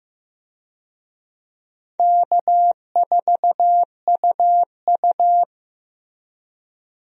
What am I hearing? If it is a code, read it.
Morse code, heard as a keyed tone, about 16 wpm, 700 Hz. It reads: K4UU
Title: USA Call Signs - 4 Characters - Repeated 2x - 15wpm